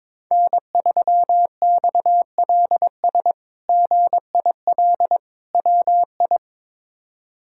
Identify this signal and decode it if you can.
Morse code, heard as a keyed tone, about 22 wpm, 700 Hz. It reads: N3XLS GIL WI
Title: K1USN SST QSO Practice - 22wpm